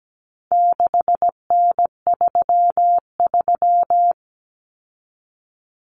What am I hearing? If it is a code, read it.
Morse code, heard as a keyed tone, about 17 wpm, 700 Hz. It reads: 6N33
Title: Common Cut Numbers - 4-Digit 17wpm